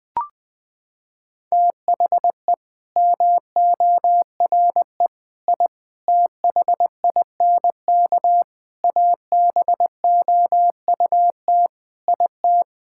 Morse code, heard as a keyed tone, about 20 wpm, 700 Hz. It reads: THE MORE I THINK ABOUT IT
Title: Sets of 8 Words - 20wpm